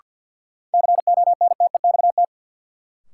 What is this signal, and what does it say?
Morse code, heard as a keyed tone, about 53 wpm, 700 Hz. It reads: CONTEXT
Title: Sentences from Top 1000 Words - Review Part 7 - New Word Drill - 50wpm